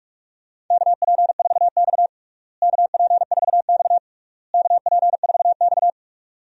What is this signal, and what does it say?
Morse code, heard as a keyed tone, about 46 wpm, 700 Hz. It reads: KP4X KP4X KP4X
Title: DXCC-100 Call Signs - 4 Characters - Repeated 3x - 45wpm